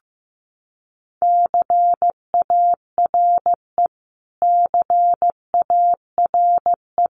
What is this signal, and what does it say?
Morse code, heard as a keyed tone, about 15 wpm, 700 Hz. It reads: CARE CARE
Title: Top 500 Words Repeated 15wpm